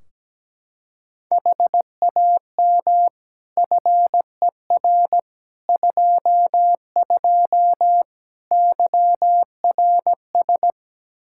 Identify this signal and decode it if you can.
Morse code, heard as a keyed tone, about 17 wpm, 700 Hz. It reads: HAM FER 22 YRS
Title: Practice QSO - Exchange 3 - More Info - 17wpm